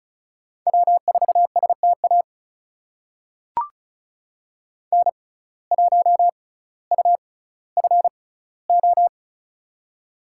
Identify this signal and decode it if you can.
Morse code, heard as a keyed tone, about 34 wpm, spaced as for 18 wpm, 700 Hz. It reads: W4STA N1UFO
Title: USA Call Signs - 5 Characters - 6x Character Spacing - 35wpm